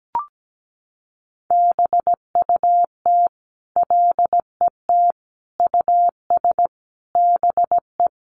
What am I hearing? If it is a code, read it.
Morse code, heard as a keyed tone, about 17 wpm, 700 Hz. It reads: BUT LET US BE
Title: Sentences from Top 1100 Words 17wpm